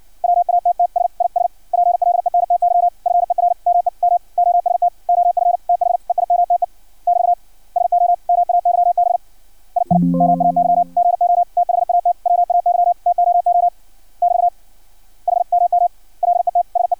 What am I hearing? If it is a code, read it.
Morse code, heard as a keyed tone, about 56 wpm, 700 Hz. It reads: WATTS IS PLENTY FER ME MOST OF THEIMTE <BT> SO MAYB II MAY GO THAT WAY TOO <BT> HMM LET SE
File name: archived / CRSnet